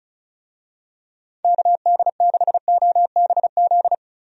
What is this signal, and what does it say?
Morse code, heard as a keyed tone, about 36 wpm, 700 Hz. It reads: KD6OBZ
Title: USA Call Signs - 6 Characters - 35wpm